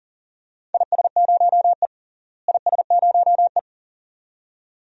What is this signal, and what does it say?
Morse code, heard as a keyed tone, about 40 wpm, 700 Hz. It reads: IS0E IS0E
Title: DXCC-100 Call Signs - 4 Characters - Repeated 2x - 40wpm